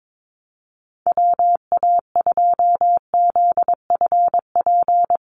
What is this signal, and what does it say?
Morse code, heard as a keyed tone, about 22 wpm, 700 Hz. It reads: WA2ZFP
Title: USA Call Signs 22wpm